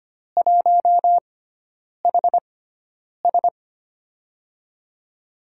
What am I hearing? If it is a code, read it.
Morse code, heard as a keyed tone, about 25 wpm, spaced as for 9 wpm, 700 Hz. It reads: 1HS
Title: USA Call Signs - 4 Characters - 6x Character Spacing - 25wpm